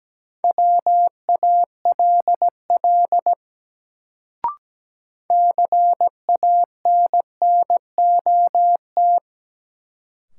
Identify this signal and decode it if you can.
Morse code, heard as a keyed tone, about 17 wpm, 700 Hz. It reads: WALL CANNOT
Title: Sentences from Top 1000 Words - Review Part 6 - New Word Drill - 17wpm